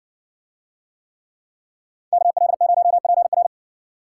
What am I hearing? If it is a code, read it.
Morse code, heard as a keyed tone, about 56 wpm, 700 Hz. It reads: KL0PR